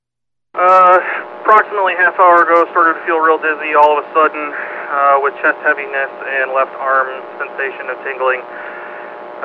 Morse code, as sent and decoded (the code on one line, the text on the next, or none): none